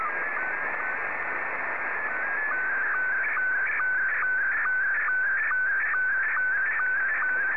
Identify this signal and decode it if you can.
SSTV (slow-scan television), Scottie 1.